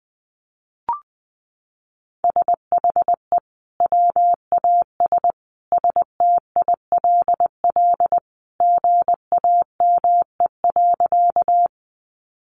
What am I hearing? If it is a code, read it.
Morse code, heard as a keyed tone, about 20 wpm, 700 Hz. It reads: SHE WAS STILL GAME.
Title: Sentences from Top 500 Words 20wpm